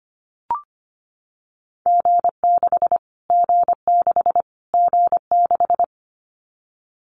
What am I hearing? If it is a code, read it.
Morse code, heard as a keyed tone, about 25 wpm, 700 Hz. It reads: G6 G6 G6
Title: Callsign Prefixes for Top 25 DXCC - Repeated 3x - 25wpm